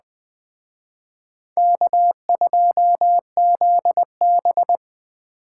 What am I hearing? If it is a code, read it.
Morse code, heard as a keyed tone, about 20 wpm, 700 Hz. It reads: K2ZB
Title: Speed Racing - USA 1x2 Call Signs 17wpm